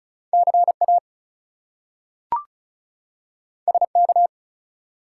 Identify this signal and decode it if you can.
Morse code, heard as a keyed tone, about 36 wpm, 700 Hz. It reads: CA SK
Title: US States and Canadian Provinces 35wpm